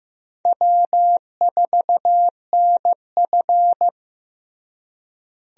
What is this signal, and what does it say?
Morse code, heard as a keyed tone, about 15 wpm, 700 Hz. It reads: W4NF
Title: USA Call Signs - 4 Characters - Repeated 2x - 15wpm